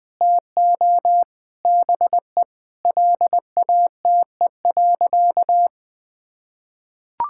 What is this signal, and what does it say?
Morse code, heard as a keyed tone, about 20 wpm, 700 Hz. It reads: TO BE LATE.
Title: Sentences with 5-Word Phrases - Spoken Beginning - 20wpm